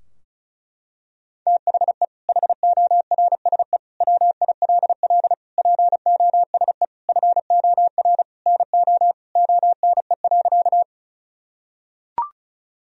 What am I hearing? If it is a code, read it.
Morse code, heard as a keyed tone, about 34 wpm, 700 Hz. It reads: THE HORSE WILL POSE FOR NO ONE.
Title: Sentences from Top 1000 Words - Review Part 2 - 35wpm